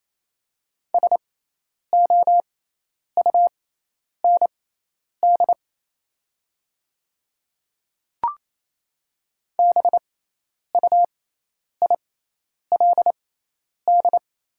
Morse code, heard as a keyed tone, about 28 wpm, spaced as for 10 wpm, 700 Hz. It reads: SOUND BUILD
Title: 6x Character Spacing - Top 300 Words - 28wpm